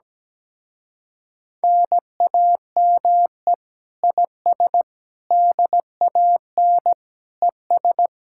Morse code, heard as a keyed tone, about 17 wpm, 700 Hz. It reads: NAME IS DAN ES
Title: Speed-Racing Common QSO Phrases - 17wpm